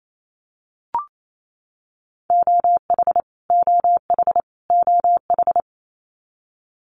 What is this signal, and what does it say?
Morse code, heard as a keyed tone, about 28 wpm, 700 Hz. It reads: OH OH OH
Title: US States Repeated 3x 28wpm